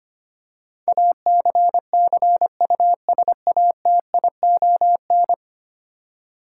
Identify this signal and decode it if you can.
Morse code, heard as a keyed tone, about 25 wpm, 700 Hz. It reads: ACCUSATION
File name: Ten Letter Words - 25wpm